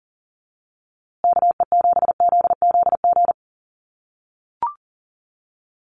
Morse code, heard as a keyed tone, about 40 wpm, 700 Hz. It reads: KE7ZZG
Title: USA Call Signs Repeated 2x 40wpm